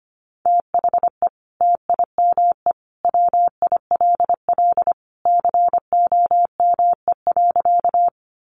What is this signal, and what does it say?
Morse code, heard as a keyed tone, about 25 wpm, 700 Hz. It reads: THE TIME WILL COME.
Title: Sentences from Top 100 Words 25wpm